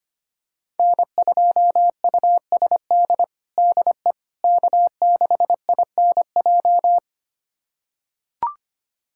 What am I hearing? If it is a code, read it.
Morse code, heard as a keyed tone, about 25 wpm, 700 Hz. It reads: N2USD DE K6INJ
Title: USA Callsign DE Callsign - 5 Characters - 25wpm